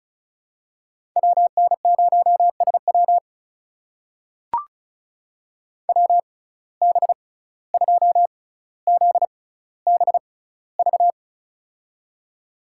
Morse code, heard as a keyed tone, about 36 wpm, spaced as for 17 wpm, 700 Hz. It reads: WN0SW WB2ZBV
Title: USA Call Signs - 6x Character Spacing - 35wpm